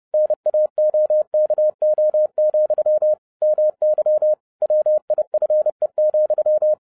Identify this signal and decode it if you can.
Morse code, heard as a keyed tone, about 30 wpm, 610 Hz. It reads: NAOKO, MY WIFE,